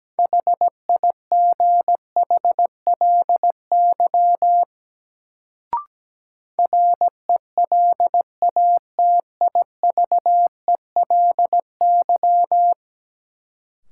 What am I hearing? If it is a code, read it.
Morse code, heard as a keyed tone, about 17 wpm, 700 Hz. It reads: HIGHLY RELATIVELY
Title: Sentences from Top 1100 Words - New Word Drill - 17wpm